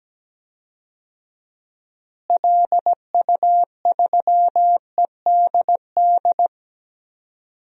Morse code, heard as a keyed tone, about 17 wpm, 700 Hz. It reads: LU3EDD
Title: DXCC-100 Call Signs - 6 Characters - Repeated 2x - 17wpm